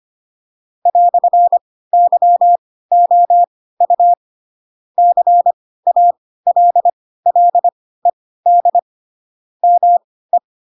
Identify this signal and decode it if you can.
Morse code, heard as a keyed tone, about 25 wpm, spaced as for 16 wpm, 700 Hz. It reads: "YOU CALLED ME
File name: Pinocchio / Pinocchio ch2 10 25 ik1hge